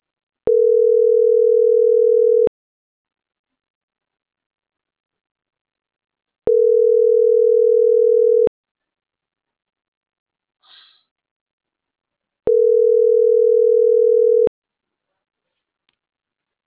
Ringback tone (North America), 440 + 480 Hz.